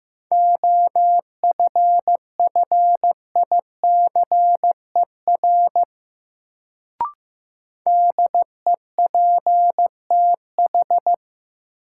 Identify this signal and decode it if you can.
Morse code, heard as a keyed tone, about 15 wpm, 700 Hz. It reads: OFFICER DEPTH
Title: Sentences from Top 1200 Words - New Word Drill Alt - 15wpm